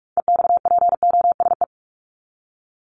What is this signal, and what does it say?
Morse code, heard as a keyed tone, about 44 wpm, 700 Hz. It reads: EXPOSE
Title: Top 1501-1750 Words 45wpm